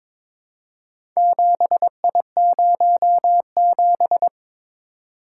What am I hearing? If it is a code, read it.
Morse code, heard as a keyed tone, about 22 wpm, 700 Hz. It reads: 7I07
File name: practice / 22wpm / Four Letter-Number Combinations - 22wpm